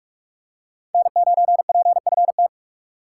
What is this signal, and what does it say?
Morse code, heard as a keyed tone, about 43 wpm, 700 Hz. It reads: N9PFT
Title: USA Call Signs - 3x Character Spacing - 45wpm